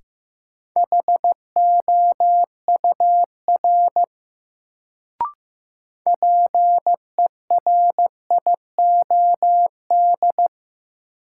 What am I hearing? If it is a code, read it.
Morse code, heard as a keyed tone, about 16 wpm, 700 Hz. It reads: HOUR PERIOD
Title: Sets of 5 Words - New Word Drill - 15wpm